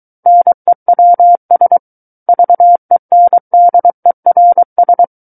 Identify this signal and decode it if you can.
Morse code, heard as a keyed tone, about 23 wpm, 700 Hz. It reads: NEWS VENDERS
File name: Book One / 14-Chapter 14 - 23 WPM